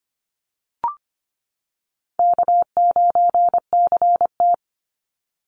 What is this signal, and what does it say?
Morse code, heard as a keyed tone, about 25 wpm, 700 Hz. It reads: K9CT
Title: Call Signs - 25wpm